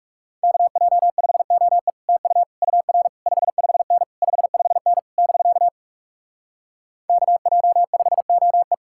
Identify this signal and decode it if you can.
Morse code, heard as a keyed tone, about 37 wpm, 700 Hz. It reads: KJ5OE TU UR 55N 55N <BK> KJ5OE
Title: Speed-Racing Common QSO Phrases - 30wpm